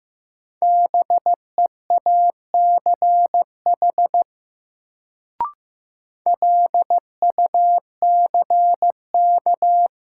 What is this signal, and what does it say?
Morse code, heard as a keyed tone, about 15 wpm, 700 Hz. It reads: BEACH LUCK
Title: Sentences from Top 1500 Words - New Word Drill Alt - 15wpm